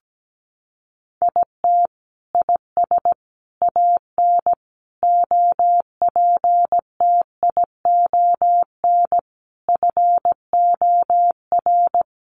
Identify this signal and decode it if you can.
Morse code, heard as a keyed tone, about 17 wpm, 700 Hz. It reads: IT IS AN OPTION FOR
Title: Sentences from Top 1300 Words 17wpm